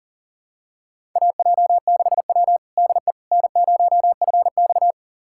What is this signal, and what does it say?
Morse code, heard as a keyed tone, about 40 wpm, 700 Hz. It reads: AJ6W DE N0FX
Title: USA Callsign DE Callsign - 4 Characters - 40wpm